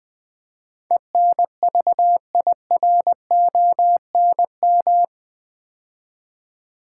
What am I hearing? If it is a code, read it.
Morse code, heard as a keyed tone, about 20 wpm, 700 Hz. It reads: ENVIRONM
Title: Build A Word - 4 Letters At A Time - 11-Letter Words - 20wpm